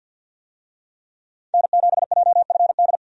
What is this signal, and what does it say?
Morse code, heard as a keyed tone, about 50 wpm, 700 Hz. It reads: N7JFD
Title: USA Call Signs - 5 Characters - 50wpm